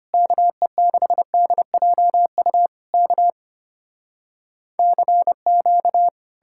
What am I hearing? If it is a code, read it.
Morse code, heard as a keyed tone, about 33 wpm, 700 Hz. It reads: KE6DJU K CQ